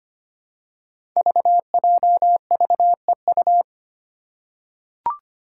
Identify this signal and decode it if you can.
Morse code, heard as a keyed tone, about 25 wpm, 700 Hz. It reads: VJVEU
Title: Five Challenging Letter-Number Combinations 25wpm